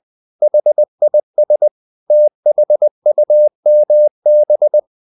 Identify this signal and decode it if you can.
Morse code, heard as a keyed tone, about 20 wpm, 600 Hz. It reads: HIS THUMB
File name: the old man and the sea / the old man and the sea part 1 20wpm